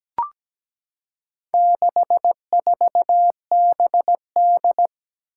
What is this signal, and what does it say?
Morse code, heard as a keyed tone, about 17 wpm, 700 Hz. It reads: 64BD
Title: Four Challenging Letter-Number Combinations 17wpm